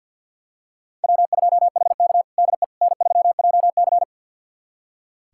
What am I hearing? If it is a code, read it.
Morse code, heard as a keyed tone, about 47 wpm, 700 Hz. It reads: W2HK DE N3JC
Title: USA Callsign DE Callsign - 4 Characters - 50wpm